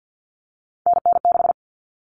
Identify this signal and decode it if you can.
Morse code, heard as a keyed tone, about 48 wpm, 700 Hz. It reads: NN6
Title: All Cut Numbers - 3-Digit 50wpm